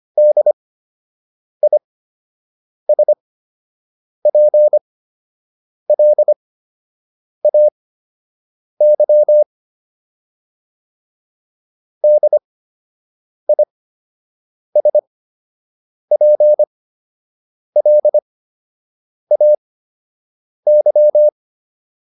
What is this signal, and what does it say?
Morse code, heard as a keyed tone, about 25 wpm, spaced as for 7 wpm, 600 Hz. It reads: DISPLAY DISPLAY
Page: fundamental-practice-files